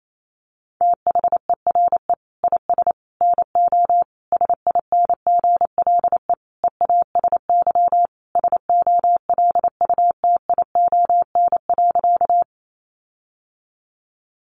Morse code, heard as a keyed tone, about 28 wpm, 700 Hz. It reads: THERE IS NO SINGLE EASY SOLUTION.
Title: Sentences from Top 1300 Words 28wpm